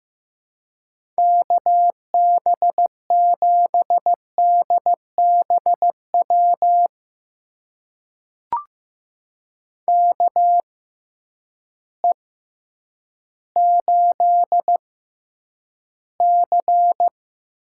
Morse code, heard as a keyed tone, about 15 wpm, spaced as for 9 wpm, 700 Hz. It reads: KB7DBW KE8C
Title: USA Call Signs - 6 Characters - 6x Character Spacing - 15wpm